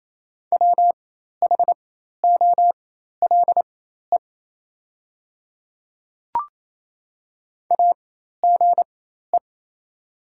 Morse code, heard as a keyed tone, about 28 wpm, spaced as for 13 wpm, 700 Hz. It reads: WHOLE AGE
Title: 4x Character Spacing - Top 500 Words - 28wpm